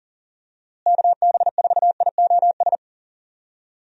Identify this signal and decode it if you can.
Morse code, heard as a keyed tone, about 40 wpm, 700 Hz. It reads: KB4IOS
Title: USA Call Signs - 6 Characters - Repeated 2x - 40wpm